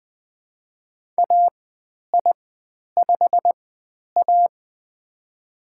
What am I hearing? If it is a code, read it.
Morse code, heard as a keyed tone, about 20 wpm, spaced as for 10 wpm, 700 Hz. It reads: AI5A
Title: 102 Lesson 17g - Letter u - Call Signs 20wpm